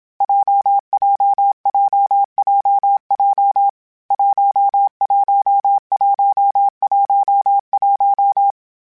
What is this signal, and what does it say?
Morse code, heard as a keyed tone, about 27 wpm, 800 Hz. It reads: JJJJJ 11111
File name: Variable Speed / Variable Speed-mp3s / WPJ WPJ1 sp @25